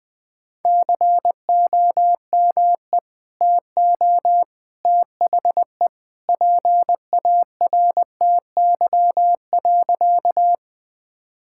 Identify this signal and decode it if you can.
Morse code, heard as a keyed tone, about 20 wpm, 700 Hz. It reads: COME TO THE PARTY.
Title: Sentences with 2-Word Phrases - Spoken Beginning - 20wpm